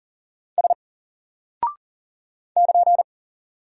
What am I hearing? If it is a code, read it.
Morse code, heard as a keyed tone, about 39 wpm, 700 Hz. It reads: S <KN>
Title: Single Character (Letter:Number:Punctuation:Prosign) 40wpm